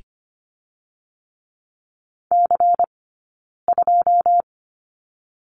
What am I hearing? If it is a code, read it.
Morse code, heard as a keyed tone, about 25 wpm, 700 Hz. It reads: C 2
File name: course / 25:10wpm / 099 Lesson 17d - Letter u - Two Characters 25wpm